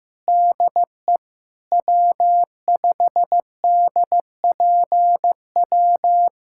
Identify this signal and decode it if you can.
Morse code, heard as a keyed tone, about 15 wpm, 700 Hz. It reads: DE W5DPW
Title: USA Callsign DE Callsign 15wpm